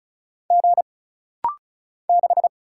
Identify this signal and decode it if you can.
Morse code, heard as a keyed tone, about 35 wpm, 700 Hz. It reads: G 6
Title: Single Letter-Number - Rapid-Fire 35wpm